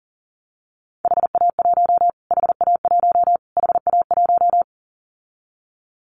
Morse code, heard as a keyed tone, about 40 wpm, 700 Hz. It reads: HA1 HA1 HA1
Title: Callsign Prefixes for Top 25 DXCC - Repeated 3x - 40wpm